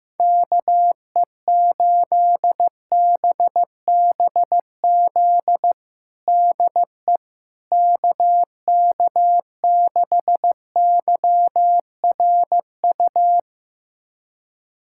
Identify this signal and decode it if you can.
Morse code, heard as a keyed tone, about 15 wpm, 700 Hz. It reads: KE8BBZ DE KK6YRU
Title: USA Callsign DE Callsign - 6 Characters - 15wpm